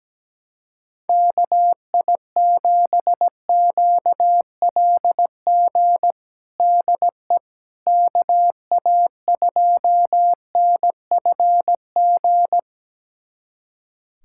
Morse code, heard as a keyed tone, about 17 wpm, 700 Hz. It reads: KI7QLG DE KA2NFG
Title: USA Callsign DE Callsign - 6 Characters - 17wpm